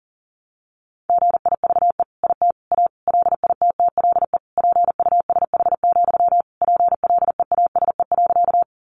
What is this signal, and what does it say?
Morse code, heard as a keyed tone, about 40 wpm, 700 Hz. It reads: GIVE IT A LITTLE PUSH, PLEASE.
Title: Sentences from Top 1500 Words 40wpm